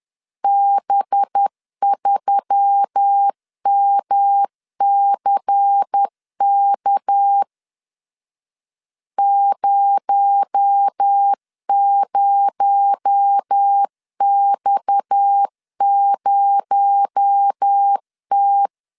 Morse code, heard as a keyed tone, about 10 wpm, 800 Hz. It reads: B3MCK 00X0T